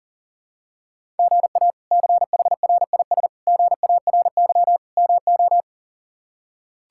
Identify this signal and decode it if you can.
Morse code, heard as a keyed tone, about 40 wpm, 700 Hz. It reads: GA CHRIS GARY MO
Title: K1USN SST QSO Practice - 40wpm